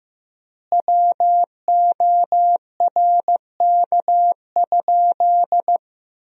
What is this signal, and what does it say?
Morse code, heard as a keyed tone, about 15 wpm, 700 Hz. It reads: WORK?
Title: 4x Word-Spacing - Sentences from Top 500 Words - 15wpm